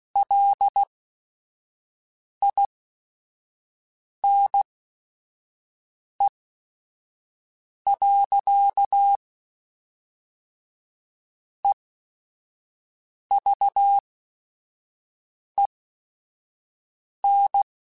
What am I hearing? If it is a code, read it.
Morse code, heard as a keyed tone, about 16 wpm, spaced as for 5 wpm, 800 Hz. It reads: LINE. EVEN
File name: Technical Subjects / Technical Subjects-mp3s / Sw Pwr Sup @5